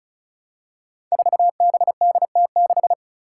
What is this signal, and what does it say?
Morse code, heard as a keyed tone, about 36 wpm, 700 Hz. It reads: 4BDT6
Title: Five Challenging Letter-Number Combinations 35wpm